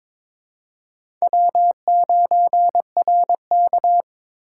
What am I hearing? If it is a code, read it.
Morse code, heard as a keyed tone, about 26 wpm, 700 Hz. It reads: W9RK
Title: Speed Racing - USA 1x2 Call Signs 22wpm